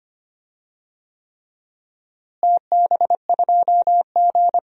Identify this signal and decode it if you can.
Morse code, heard as a keyed tone, about 25 wpm, 700 Hz. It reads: TB2G
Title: DXCC-100 Call Signs - 4 Characters - Repeated 3x - 25wpm